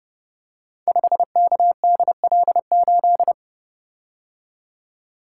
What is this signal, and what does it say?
Morse code, heard as a keyed tone, about 30 wpm, 700 Hz. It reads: 5KDL8
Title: Code Groups - 5 Letter:Numbers - 30wpm